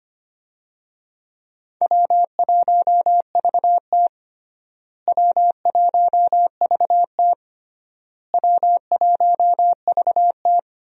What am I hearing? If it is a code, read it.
Morse code, heard as a keyed tone, about 25 wpm, 700 Hz. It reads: W1VT W1VT W1VT
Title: USA Call Signs - 4 Characters - Repeated 3x - 25wpm